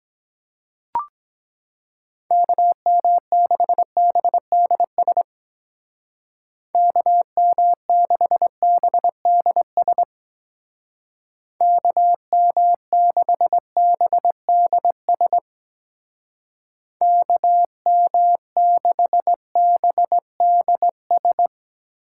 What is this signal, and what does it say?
Morse code, heard as a keyed tone, about 22 wpm, 700 Hz. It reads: KM6BDS KM6BDS KM6BDS KM6BDS